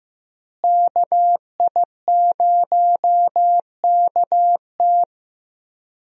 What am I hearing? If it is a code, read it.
Morse code, heard as a keyed tone, about 15 wpm, 700 Hz. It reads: KI0KT
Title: USA Call Signs - 5 Characters - Repeated 3x - 15wpm